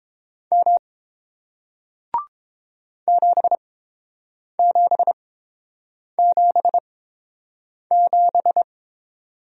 Morse code, heard as a keyed tone, about 30 wpm, spaced as for 7 wpm, 700 Hz. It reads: M 7777